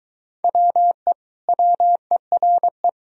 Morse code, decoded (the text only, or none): WE WERE